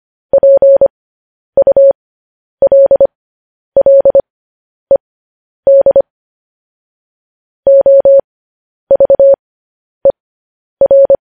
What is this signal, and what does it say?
Morse code, heard as a keyed tone, about 25 wpm, spaced as for 10 wpm, 550 Hz. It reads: PULLED OVER
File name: podcast / News headlines-1x-25x10